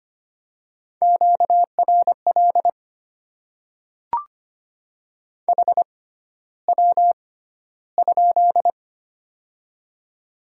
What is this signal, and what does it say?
Morse code, heard as a keyed tone, about 25 wpm, spaced as for 13 wpm, 700 Hz. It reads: QRL HW?
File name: Top 20 QSO Elements - 6x Character Spacing - 25wpm